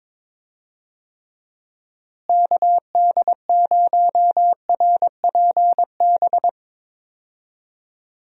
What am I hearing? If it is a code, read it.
Morse code, heard as a keyed tone, about 22 wpm, 700 Hz. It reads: KD0RPB
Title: USA Call Signs - 6 Characters - Repeated 3x - 22wpm